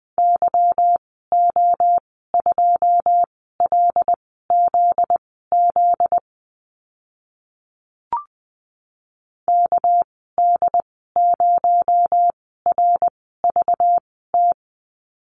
Morse code, heard as a keyed tone, about 20 wpm, spaced as for 15 wpm, 700 Hz. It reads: YO2LZZ KD0RVT
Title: 245 Lesson 39g - Letter z - Call Signs 20wpm